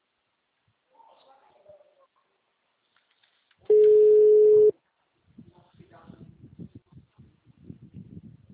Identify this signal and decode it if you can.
Ringback tone (Europe, 425 Hz), with speech.